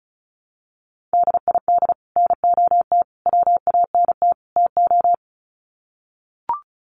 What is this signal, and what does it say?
Morse code, heard as a keyed tone, about 34 wpm, 700 Hz. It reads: DID NOT WANT TO
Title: Sets of 4 Words 35wpm